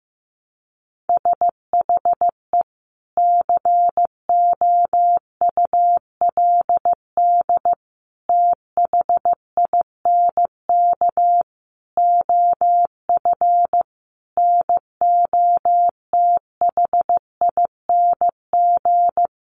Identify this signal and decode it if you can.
Morse code, heard as a keyed tone, about 15 wpm, 700 Hz. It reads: SHE COULD THINK OF NOTHING
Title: Sentences from Top 500 Words 15wpm